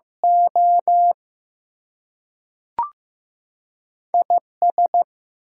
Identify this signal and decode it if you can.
Morse code, heard as a keyed tone, about 15 wpm, 700 Hz. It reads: O IS
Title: Two Letter Words - 15wpm